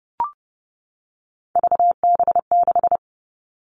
Morse code, heard as a keyed tone, about 32 wpm, 700 Hz. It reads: VB6
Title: Three Challenging Letter-Number Combinations 30wpm